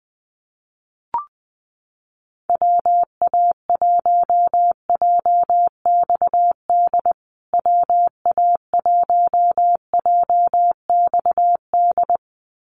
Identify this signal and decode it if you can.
Morse code, heard as a keyed tone, about 20 wpm, 700 Hz. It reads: WA1JXD WA1JXD
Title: USA Call Signs Repeated 2x 20wpm